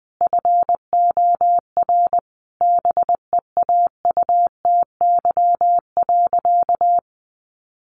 Morse code, heard as a keyed tone, about 20 wpm, 700 Hz. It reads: FOR BEAUTY.
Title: Sentences from Top 1500 Words 20wpm